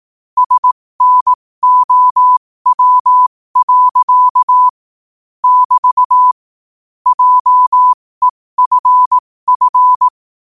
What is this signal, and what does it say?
Morse code, heard as a keyed tone, about 18 wpm, spaced as for 15 wpm, 1000 Hz. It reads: SNOW. <BT> JEFF